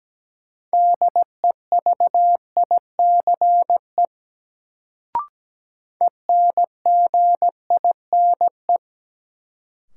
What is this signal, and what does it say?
Morse code, heard as a keyed tone, about 17 wpm, 700 Hz. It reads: DEVICE ENGINE
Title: Sentences from Top 1500 Words - New Word Drill Alt - 17wpm